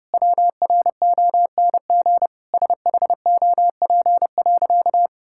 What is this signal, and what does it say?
Morse code, heard as a keyed tone, about 30 wpm, 700 Hz. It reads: WRONG SHOP.
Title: Sentences from Top 1600 Words 30wpm